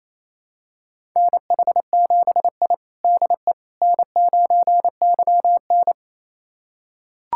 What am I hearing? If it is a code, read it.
Morse code, heard as a keyed tone, about 27 wpm, 700 Hz. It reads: NH7I DE N9YN
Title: USA Callsign DE Callsign - 4 Characters - 28wpm